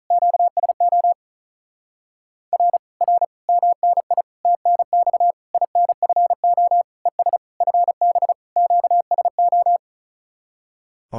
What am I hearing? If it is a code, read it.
Morse code, heard as a keyed tone, about 39 wpm, 700 Hz. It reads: QSO R R MNI TNX INFO ES FB QSO
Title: Speed-Racing Common QSO Phrases - 35wpm